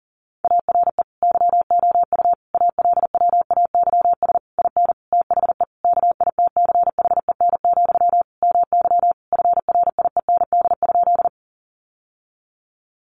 Morse code, heard as a keyed tone, about 40 wpm, 700 Hz. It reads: ARE YOU ALWAYS IN THE KITCHEN, MY FRIEND?
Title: Sentences from Top 1400 Words 40wpm